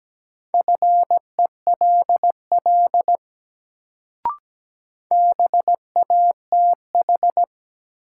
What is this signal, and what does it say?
Morse code, heard as a keyed tone, about 17 wpm, 700 Hz. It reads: FELL BATH
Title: Sentences from Top 1600 Words - New Word Drill - 17wpm